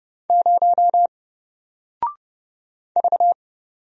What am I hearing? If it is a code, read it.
Morse code, heard as a keyed tone, about 30 wpm, 700 Hz. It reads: 0 V
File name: Single-Letter-Number-30wpm